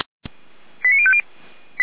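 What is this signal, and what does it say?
Five-tone selective call (ZVEI), heard as quick runs of five tones, tones of about 70 ms. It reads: 89050
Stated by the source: SDR reception of ZVEI five-tone call 89276, from the feuerwehr-orlamuende.org fire brigade server.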